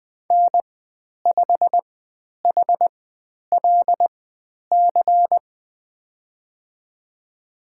Morse code, heard as a keyed tone, about 20 wpm, spaced as for 10 wpm, 700 Hz. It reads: N5HLC
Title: 095 Lesson 16g - Letter c - Call Signs 20wpm